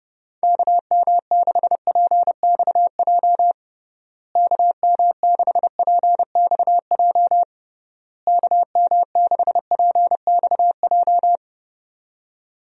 Morse code, heard as a keyed tone, about 30 wpm, 700 Hz. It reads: KM6PXJ KM6PXJ KM6PXJ
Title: USA Call Signs - 6 Characters - Repeated 3x - 30wpm